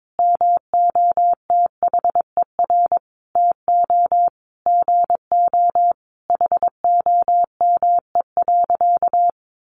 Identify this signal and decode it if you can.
Morse code, heard as a keyed tone, about 22 wpm, 700 Hz. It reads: MOTHER TO GO HOME.
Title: Sentences from Top 1400 Words 22wpm